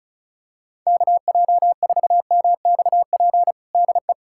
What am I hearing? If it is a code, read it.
Morse code, heard as a keyed tone, about 36 wpm, 690 Hz. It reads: KJ4MXP DE
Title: USA Callsign DE Callsign 35wpm